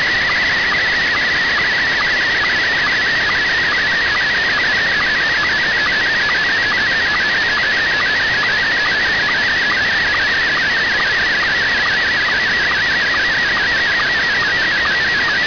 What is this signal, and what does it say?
Two-tone teleprinter signal, about 200 Hz shift.
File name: noisy